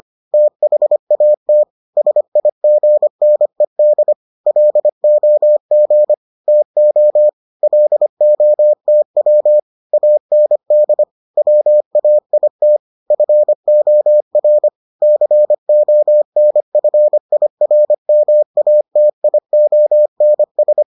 Morse code, heard as a keyed tone, about 25 wpm, 600 Hz. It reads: THAT SIGNED LOG TO LOTW AND WAIT FOR CONFIRMATIONS